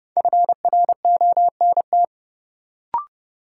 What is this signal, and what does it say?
Morse code, heard as a keyed tone, about 30 wpm, 700 Hz. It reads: FRONT
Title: Sentences from Top 500 Words - New Word Drill - 30wpm